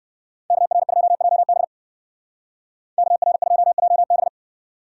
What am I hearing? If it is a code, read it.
Morse code, heard as a keyed tone, about 53 wpm, 700 Hz. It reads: KR2JB KR2JB
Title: Speed Racing - USA 2x2 Call Signs 45wpm